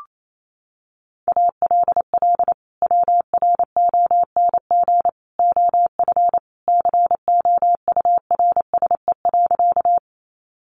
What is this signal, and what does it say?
Morse code, heard as a keyed tone, about 28 wpm, 700 Hz. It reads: ALL WRONG OF COURSE.
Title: Sentences from Top 900 Words 28wpm